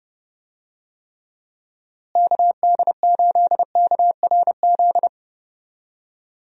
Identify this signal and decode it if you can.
Morse code, heard as a keyed tone, about 32 wpm, 700 Hz. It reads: KD8KRZ
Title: USA Call Signs - 6 Characters - Repeated 2x - 30wpm